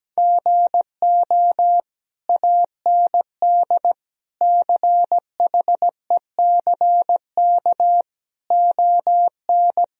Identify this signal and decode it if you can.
Morse code, heard as a keyed tone, about 17 wpm, 700 Hz. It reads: GO AND CHECK ON
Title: Sentences from Top 1000 Words - Review Part 3 - 17wpm